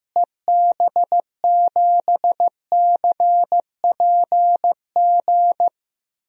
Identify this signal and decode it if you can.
Morse code, heard as a keyed tone, about 15 wpm, 700 Hz. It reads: EB7CPG
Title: DXCC-100 Call Signs - 6 Characters - Repeated 2x - 15wpm